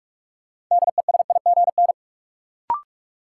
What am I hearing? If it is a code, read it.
Morse code, heard as a keyed tone, about 46 wpm, 700 Hz. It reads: DESIGN
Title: Sentences from Top 800 Words - New Word Drill - 45wpm